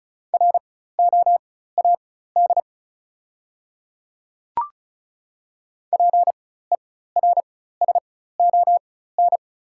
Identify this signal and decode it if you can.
Morse code, heard as a keyed tone, about 36 wpm, spaced as for 16 wpm, 700 Hz. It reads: ROAD PERSON
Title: 4x Character Spacing - Top 500 Words - 35wpm